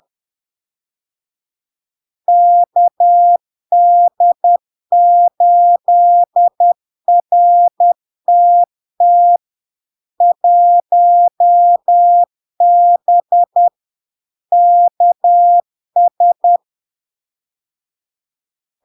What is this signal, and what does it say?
Morse code, heard as a keyed tone, about 10 wpm, 700 Hz. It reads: KD8RTT 1B KS
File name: LARK-FD-2022-Log-in-CW-10wpm-700-Hz